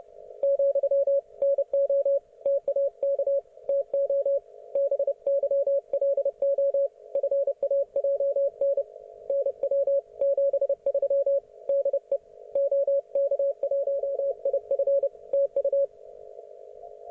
Morse code, heard as a keyed tone, about 29 wpm, 560 Hz. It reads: , NO TAK TO BYLO FAJN NW 73 DE OK1IF TU
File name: Archiv / CUC133 / oK1fhi cul133